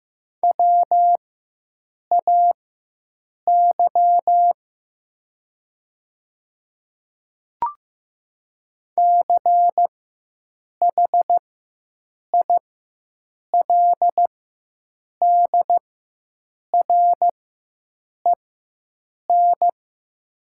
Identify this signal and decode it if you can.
Morse code, heard as a keyed tone, about 16 wpm, spaced as for 7 wpm, 700 Hz. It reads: WAY CHILDREN
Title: 4x Character Spacing - Top 300 Words - 15wpm